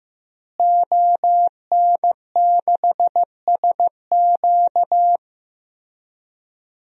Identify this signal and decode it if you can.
Morse code, heard as a keyed tone, about 16 wpm, 700 Hz. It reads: ON6SQ
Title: DXCC-100 Call Signs - 5 Characters - Repeated 3x - 15wpm